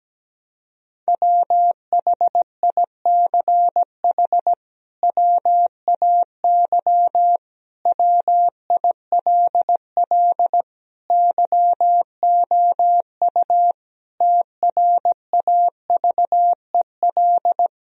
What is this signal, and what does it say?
Morse code, heard as a keyed tone, about 17 wpm, 700 Hz. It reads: WHICH WAY WILL YOU TRAVEL
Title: Sentences from Top 1000 Words - Review Part 2 - 17wpm